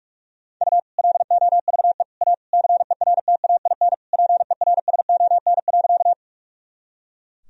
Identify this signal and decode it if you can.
Morse code, heard as a keyed tone, about 46 wpm, 690 Hz. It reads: U LOVE A CERTAIN PERSON.
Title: Sentences from Top 500 Words and Abbreviations 45wpm